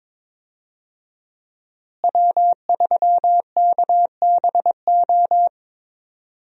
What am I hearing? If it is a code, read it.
Morse code, heard as a keyed tone, about 22 wpm, 700 Hz. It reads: W3KBO
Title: USA Call Signs - 5 Characters - Repeated 3x - 22wpm